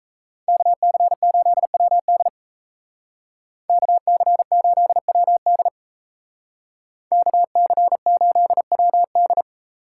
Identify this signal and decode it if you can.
Morse code, heard as a keyed tone, about 38 wpm, 700 Hz. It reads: KC8WD KC8WD KC8WD